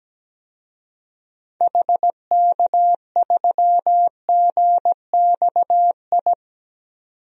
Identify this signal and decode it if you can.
Morse code, heard as a keyed tone, about 17 wpm, 700 Hz. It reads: HK3GXI
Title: DXCC-100 Call Signs - 6 Characters - 17wpm